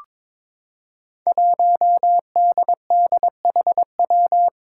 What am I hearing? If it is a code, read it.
Morse code, heard as a keyed tone, about 21 wpm, 700 Hz. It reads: 1DDHW
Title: Five Challenging Letter-Number Combinations 22wpm